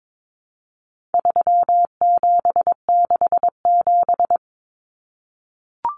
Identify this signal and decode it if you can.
Morse code, heard as a keyed tone, about 22 wpm, 700 Hz. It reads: 3767
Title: Numbers - 4-Digit 22wpm